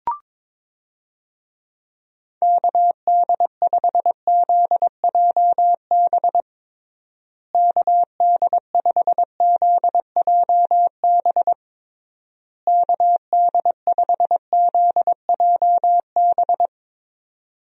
Morse code, heard as a keyed tone, about 22 wpm, 700 Hz. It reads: KD5ZJB KD5ZJB KD5ZJB
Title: USA Call Signs - 6 Characters - Repeated 3x - 22wpm